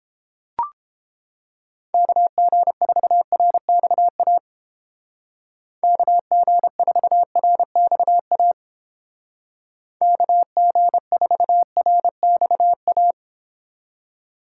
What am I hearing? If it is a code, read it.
Morse code, heard as a keyed tone, about 29 wpm, 700 Hz. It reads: KG4RXA KG4RXA KG4RXA